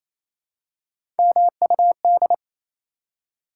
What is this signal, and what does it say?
Morse code, heard as a keyed tone, about 28 wpm, 700 Hz. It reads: MUD
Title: Three Letter Words - 28wpm